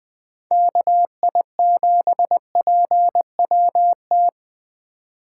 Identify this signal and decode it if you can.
Morse code, heard as a keyed tone, about 20 wpm, 700 Hz. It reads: KI7PWT
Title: Speed Racing - USA 2x3 Call Signs 17wpm